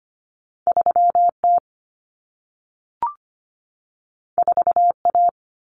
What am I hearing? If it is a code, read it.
Morse code, heard as a keyed tone, about 25 wpm, 700 Hz. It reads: 3T 4A
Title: Common Cut Numbers - 2-Digit 25wpm